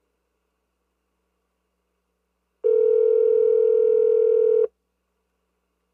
Ringback tone (North America), 440 + 480 Hz.
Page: car-&amp;-cell-phones|futz|touch-tone-dialing-&amp;-ringing